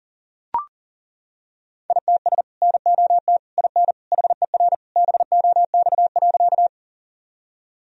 Morse code, heard as a keyed tone, about 40 wpm, 700 Hz. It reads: ITS NOT IN HER BOX.
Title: Sentences from Top 1000 Words - Review Part 2 - 40wpm